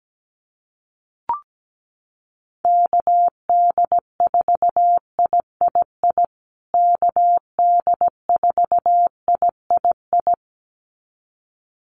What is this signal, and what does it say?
Morse code, heard as a keyed tone, about 17 wpm, 700 Hz. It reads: KD4III KD4III
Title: USA Call Signs Repeated 2x 17wpm